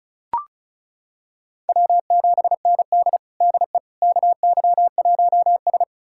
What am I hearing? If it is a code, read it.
Morse code, heard as a keyed tone, about 36 wpm, 700 Hz. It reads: W7ND DE KY1S
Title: USA Callsign DE Callsign - 4 Characters - 35wpm